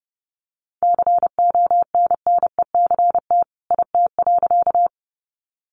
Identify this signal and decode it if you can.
Morse code, heard as a keyed tone, about 30 wpm, 700 Hz. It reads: CONNECT IT.
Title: Sentences from Top 1500 Words 30wpm